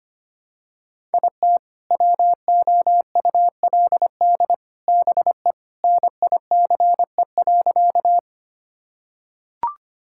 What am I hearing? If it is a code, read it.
Morse code, heard as a keyed tone, about 25 wpm, 700 Hz. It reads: IT WOULD BE NICE.
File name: practice / 25wpm / Sentences with 3-Word Phrases - 25wpm